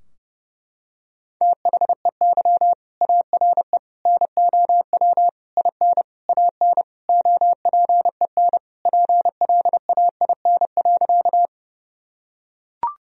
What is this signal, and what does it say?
Morse code, heard as a keyed tone, about 30 wpm, 700 Hz. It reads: THEY ARE NOW IN AN OPEN PLAIN.
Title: Sentences from Top 1000 Words - Review Part 2 - 30wpm